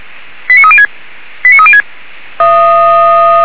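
Five-tone selective call (ZVEI), heard as quick runs of five tones, tones of about 70 ms. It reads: 89306 89306
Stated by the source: ZVEI five-tone call received on an SDR.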